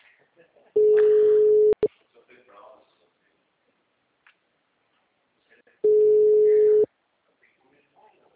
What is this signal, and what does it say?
Ringback tone (Europe, 425 Hz), with speech.
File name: out- -2005- - - .35099